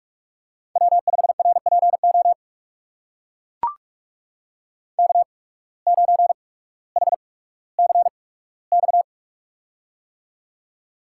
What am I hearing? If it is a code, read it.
Morse code, heard as a keyed tone, about 43 wpm, spaced as for 19 wpm, 700 Hz. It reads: W5RPO K9HCX
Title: USA Call Signs - 5 Characters - 8x Character Spacing - 45wpm